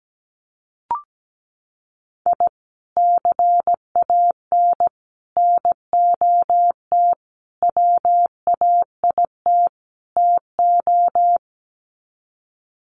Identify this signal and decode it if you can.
Morse code, heard as a keyed tone, about 17 wpm, 700 Hz. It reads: I CAN NOT WAIT TO
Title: Sets of 5 Words Encore 17wpm